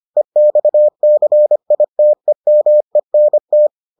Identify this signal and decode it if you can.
Morse code, heard as a keyed tone, about 25 wpm, 600 Hz. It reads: EXCITEMENT